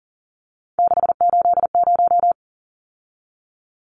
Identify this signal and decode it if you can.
Morse code, heard as a keyed tone, about 40 wpm, 700 Hz. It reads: 680
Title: Numbers - 3-Digit 40wpm